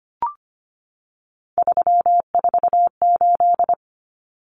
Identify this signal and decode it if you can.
Morse code, heard as a keyed tone, about 25 wpm, 700 Hz. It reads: 348
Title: Common Cut Numbers - 3-Digit 25wpm